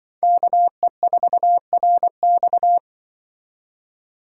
Morse code, heard as a keyed tone, about 24 wpm, 700 Hz. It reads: KE4RX
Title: Speed Racing - USA 2x2 Call Signs 20wpm